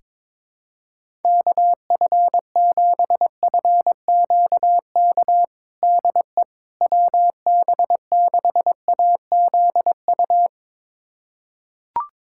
Morse code, heard as a keyed tone, about 22 wpm, 700 Hz. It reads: KF7FQK DE WB6AZU
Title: USA Callsign DE Callsign - 6 Characters - 22wpm